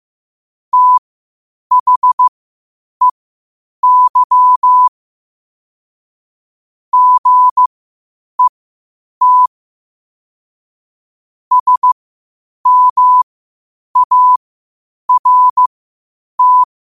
Morse code, heard as a keyed tone, about 15 wpm, spaced as for 8 wpm, 1000 Hz. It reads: THEY GET SMART
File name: qotdfiles / 8wpm